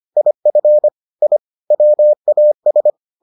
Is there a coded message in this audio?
Yes: Morse code from 0.2 to 2.9 s, heard as a keyed tone, about 25 wpm, 600 Hz. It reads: IF I WAS